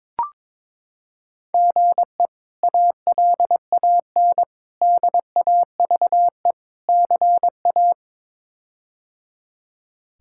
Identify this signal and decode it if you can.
Morse code, heard as a keyed tone, about 22 wpm, 700 Hz. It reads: GE ALAN DAVE CA
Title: K1USN SST QSO Practice - 22wpm